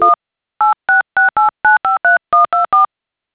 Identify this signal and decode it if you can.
DTMF, heard as telephone touch tones, tones of about 140 ms. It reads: 1 8668953124